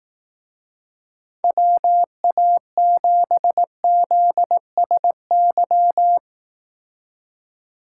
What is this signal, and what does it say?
Morse code, heard as a keyed tone, about 18 wpm, 700 Hz. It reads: WA7ZSY
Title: Speed Racing - USA 2x3 Call Signs 15wpm